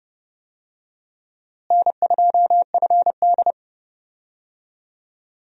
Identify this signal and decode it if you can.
Morse code, heard as a keyed tone, about 32 wpm, 700 Hz. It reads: N2FD
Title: USA Call Signs - 4 Characters - Repeated 3x - 30wpm